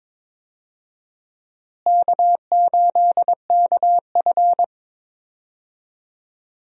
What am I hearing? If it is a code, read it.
Morse code, heard as a keyed tone, about 22 wpm, 700 Hz. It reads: K8KF
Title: USA Call Signs - 4 Characters - Repeated 2x - 22wpm